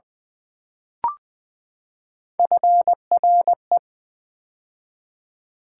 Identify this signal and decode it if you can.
Morse code, heard as a keyed tone, about 20 wpm, 700 Hz. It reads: FRE
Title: Build A Word - 3 Letters At A Time - 8-Letter Words - 20wpm